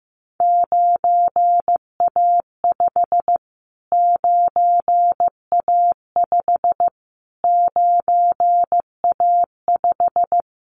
Morse code, heard as a keyed tone, about 15 wpm, 700 Hz. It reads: 9A5 9A5 9A5
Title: Callsign Prefixes for Top 25 DXCC - Repeated 3x - 15wpm